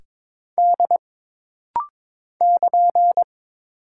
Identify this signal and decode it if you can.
Morse code, heard as a keyed tone, about 21 wpm, 700 Hz. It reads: D <KN>
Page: This rapid-fire timing is geared to push your ICR (Instant Character Recognition) skills.